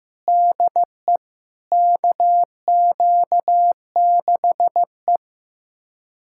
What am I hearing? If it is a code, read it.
Morse code, heard as a keyed tone, about 15 wpm, 700 Hz. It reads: DE KQ6E
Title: USA Callsign DE Callsign - 4 Characters - 15wpm